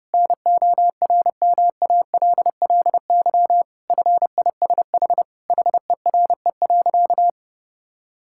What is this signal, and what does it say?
Morse code, heard as a keyed tone, about 28 wpm, 700 Hz. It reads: NORMALLY FISH HERE.
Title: Sentences from Top 1000 Words - Review Part 6 - 30wpm